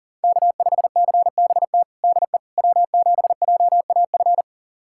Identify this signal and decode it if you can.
Morse code, heard as a keyed tone, about 40 wpm, 700 Hz. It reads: K5CBT DE W7JAF
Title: USA Callsign DE Callsign - 5 Characters - 40wpm